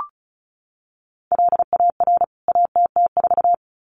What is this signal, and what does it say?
Morse code, heard as a keyed tone, about 34 wpm, 700 Hz. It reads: LAR ATT4
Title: CWT Exchange 35wpm